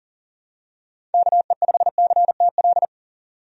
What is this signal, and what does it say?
Morse code, heard as a keyed tone, about 40 wpm, 700 Hz. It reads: KE5CTL
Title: USA Call Signs - 6 Characters - 40wpm